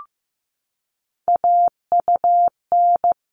Morse code, heard as a keyed tone, about 15 wpm, 700 Hz. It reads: AUN
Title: All Cut Numbers - 3-Digit 15wpm